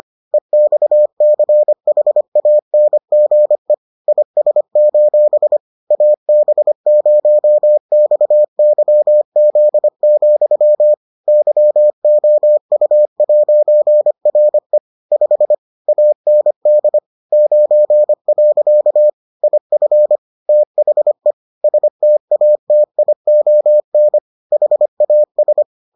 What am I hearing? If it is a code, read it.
Morse code, heard as a keyed tone, about 25 wpm, 600 Hz. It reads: EXCHANGE IS: AB0XYZ, YOU'RE 5 AND 9. IF THE STATION HAS